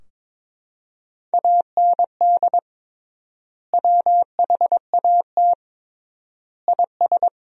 Morse code, heard as a keyed tone, about 22 wpm, 700 Hz. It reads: AND WHAT IS
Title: 3x Word-Spacing - Sentences from Top 200 Words - 22wpm